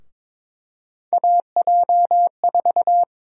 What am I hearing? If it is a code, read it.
Morse code, heard as a keyed tone, about 22 wpm, 700 Hz. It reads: AJ4
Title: USA Call Sign Prefixes - Repeated 2x - 22wpm